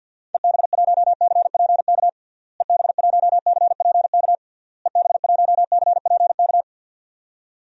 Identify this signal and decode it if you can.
Morse code, heard as a keyed tone, about 47 wpm, 700 Hz. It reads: EB1CPK EB1CPK EB1CPK
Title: DXCC-100 Call Signs - 6 Characters - Repeated 3x - 50wpm